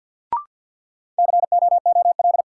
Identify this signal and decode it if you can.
Morse code, heard as a keyed tone, about 50 wpm, 700 Hz. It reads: COOL
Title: Sentences from Top 1200 Words - New Word Drill - 50wpm